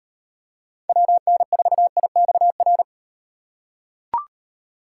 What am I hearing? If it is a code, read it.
Morse code, heard as a keyed tone, about 38 wpm, 700 Hz. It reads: WN4IXR